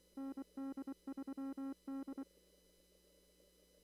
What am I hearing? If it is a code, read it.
Morse code, heard as a keyed tone, about 24 wpm, 270 Hz. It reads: ND3D
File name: CW / kt5j wpx cw-op-k5pi